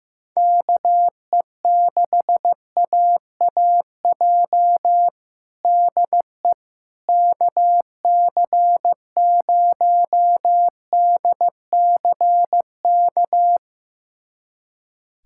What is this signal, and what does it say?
Morse code, heard as a keyed tone, about 15 wpm, 700 Hz. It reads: KE6AAJ DE KC0DCK
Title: USA Callsign DE Callsign - 6 Characters - 15wpm